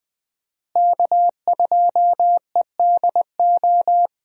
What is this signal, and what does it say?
Morse code, heard as a keyed tone, about 20 wpm, 700 Hz. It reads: K2EDO
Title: USA Call Signs - 5 Characters - Repeated 3x - 20wpm